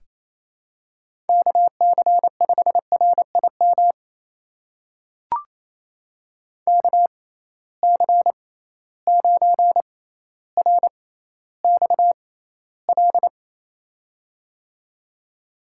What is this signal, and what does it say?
Morse code, heard as a keyed tone, about 27 wpm, spaced as for 14 wpm, 700 Hz. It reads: KC5RIM KC9RXL
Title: USA Call Signs - 6 Characters - 6x Character Spacing - 28wpm